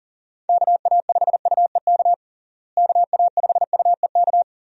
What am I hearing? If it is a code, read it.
Morse code, heard as a keyed tone, about 40 wpm, 700 Hz. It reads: KA5UEK KA5UEK
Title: USA Call Signs - 6 Characters - Repeated 3x - 40wpm